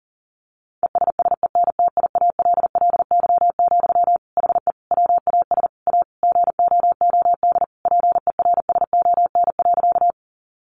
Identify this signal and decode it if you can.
Morse code, heard as a keyed tone, about 40 wpm, 700 Hz. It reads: ESSENTIALLY, HE WAS A GOOD PERSON.
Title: Sentences from Top 1300 Words 40wpm